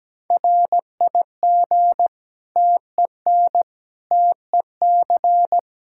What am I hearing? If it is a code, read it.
Morse code, heard as a keyed tone, about 17 wpm, 700 Hz. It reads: RIG TEN TEC
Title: Practice QSO - Exchange 2 - Info - 17wpm